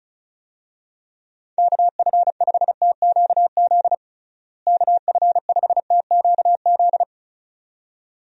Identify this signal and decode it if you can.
Morse code, heard as a keyed tone, about 36 wpm, 700 Hz. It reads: KF5TQZ KF5TQZ
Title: USA Call Signs - 6 Characters - Repeated 2x - 35wpm